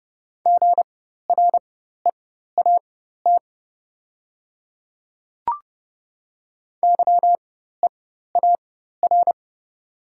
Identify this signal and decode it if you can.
Morse code, heard as a keyed tone, about 30 wpm, spaced as for 14 wpm, 700 Hz. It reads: GREAT YEAR
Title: 4x Character Spacing - Top 300 Words - 30wpm